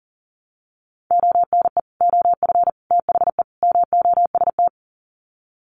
Morse code, heard as a keyed tone, about 39 wpm, 700 Hz. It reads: ONE OF THE MOST
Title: Sets of 4 Words 40wpm